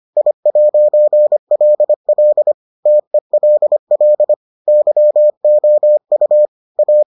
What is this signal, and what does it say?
Morse code, heard as a keyed tone, about 25 wpm, 600 Hz. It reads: I'LL TELL YOU A